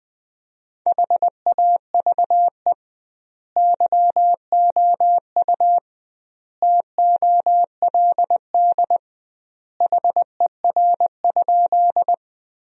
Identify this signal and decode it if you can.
Morse code, heard as a keyed tone, about 21 wpm, 700 Hz. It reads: HAVE YOU TOLD HER?
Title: 2x Word-Spacing - Sentences from Top 400 Words - 20wpm